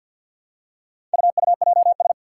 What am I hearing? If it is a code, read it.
Morse code, heard as a keyed tone, about 50 wpm, 700 Hz. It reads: UUJS
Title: Four Challenging Letter-Number Combinations 50wpm